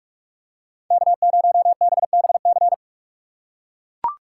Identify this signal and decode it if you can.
Morse code, heard as a keyed tone, about 44 wpm, 700 Hz. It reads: K0BBC
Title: USA Call Signs - 5 Characters - 6x Character Spacing - 45wpm